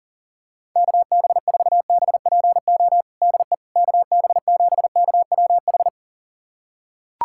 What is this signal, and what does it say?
Morse code, heard as a keyed tone, about 37 wpm, 700 Hz. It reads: KB4BPO DE KB7KWH
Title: USA Callsign DE Callsign - 6 Characters - 40wpm